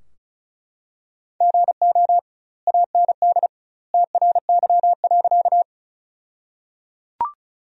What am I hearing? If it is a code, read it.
Morse code, heard as a keyed tone, about 34 wpm, 700 Hz. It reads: GO AND TRY.
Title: 2x Word-Spacing - Sentences from Top 200 Words - 35wpm